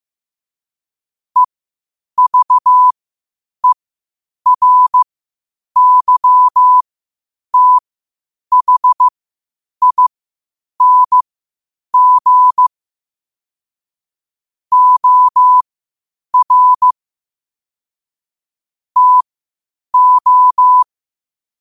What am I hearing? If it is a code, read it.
Morse code, heard as a keyed tone, about 15 wpm, spaced as for 8 wpm, 1000 Hz. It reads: EVERYTHING OR TO
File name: qotdfiles / 8wpm